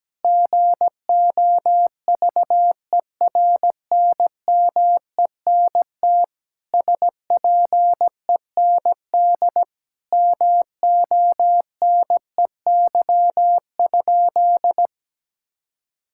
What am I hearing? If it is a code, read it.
Morse code, heard as a keyed tone, about 17 wpm, 700 Hz. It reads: GOVERNMENT SPEND MONEY?
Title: Sentences from Top 1000 Words - Review Part 5 - 17wpm